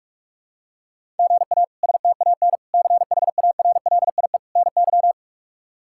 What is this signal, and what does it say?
Morse code, heard as a keyed tone, about 46 wpm, 700 Hz. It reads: GA STAN CHARLIE NY
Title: K1USN SST QSO Practice - 45wpm